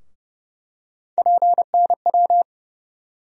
Morse code, heard as a keyed tone, about 30 wpm, 700 Hz. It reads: PNW
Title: USA Call Sign Postfixes - 3 Letters 30wpm